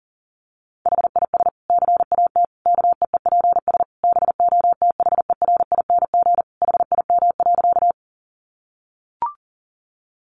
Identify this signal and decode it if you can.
Morse code, heard as a keyed tone, about 40 wpm, 700 Hz. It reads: HIS CAT KEEPS BOTHERING HIM.
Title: Additional Easy Phrases 40wpm